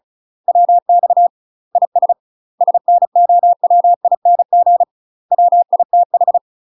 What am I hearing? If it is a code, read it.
Morse code, heard as a keyed tone, about 34 wpm, 700 Hz. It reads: WX IS SNOWING WITH
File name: UK MASTER QSOs TRAINING / 35WPM / UK MASTER QSOs 0025